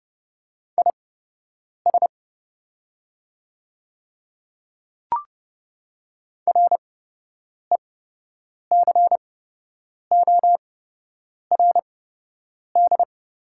Morse code, heard as a keyed tone, about 32 wpm, spaced as for 8 wpm, 700 Hz. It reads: IS RECORD
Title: 8x Character Spacing - Top 500 Words - 30wpm